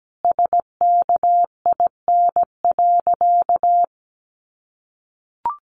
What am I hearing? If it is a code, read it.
Morse code, heard as a keyed tone, about 17 wpm, 700 Hz. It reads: SKIN.
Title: Sentences from Top 1300 Words 17wpm